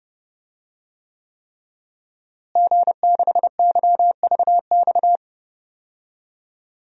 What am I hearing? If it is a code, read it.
Morse code, heard as a keyed tone, about 32 wpm, 700 Hz. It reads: G6YVX
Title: DXCC-100 Call Signs - 5 Characters - Repeated 3x - 30wpm